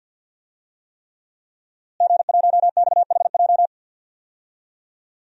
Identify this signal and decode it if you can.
Morse code, heard as a keyed tone, about 52 wpm, 700 Hz. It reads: G1XHJ
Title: DXCC-100 Call Signs - 5 Characters - Repeated 3x - 50wpm